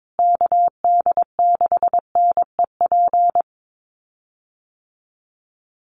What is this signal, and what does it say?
Morse code, heard as a keyed tone, about 22 wpm, 700 Hz. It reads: KD6NEP
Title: USA Call Signs Repeated 3x 22wpm